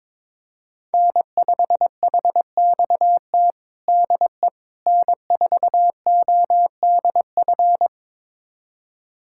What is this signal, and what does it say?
Morse code, heard as a keyed tone, about 21 wpm, 700 Hz. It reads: N5HXT DE N4ODF
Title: USA Callsign DE Callsign - 5 Characters - 22wpm